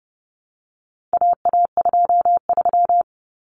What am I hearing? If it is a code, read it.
Morse code, heard as a keyed tone, about 30 wpm, 700 Hz. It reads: AA23
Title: Common Cut Numbers - 4-Digit 30wpm